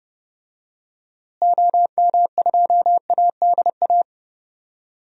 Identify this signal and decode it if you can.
Morse code, heard as a keyed tone, about 30 wpm, 700 Hz. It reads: OM2ADA
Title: DXCC-100 Call Signs - 6 Characters - Repeated 3x - 30wpm